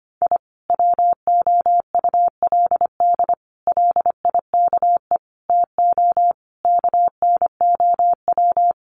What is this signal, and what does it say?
Morse code, heard as a keyed tone, about 25 wpm, 700 Hz. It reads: I WOULD LIKE TO KNOW
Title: Sets of 5 Words Encore 25wpm